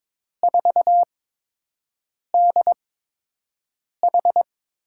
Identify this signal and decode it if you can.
Morse code, heard as a keyed tone, about 22 wpm, spaced as for 6 wpm, 700 Hz. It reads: 4DH
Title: USA Call Signs - 8x Character Spacing - 22wpm